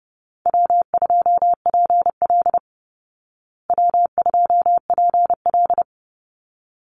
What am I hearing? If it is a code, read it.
Morse code, heard as a keyed tone, about 30 wpm, 700 Hz. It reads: W2PL W2PL
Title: Easy Call Signs Repeated 2x - 30wpm